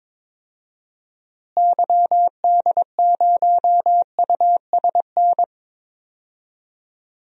Morse code, heard as a keyed tone, about 22 wpm, 700 Hz. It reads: YD0USN
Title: DXCC-100 Call Signs - 6 Characters - Repeated 3x - 22wpm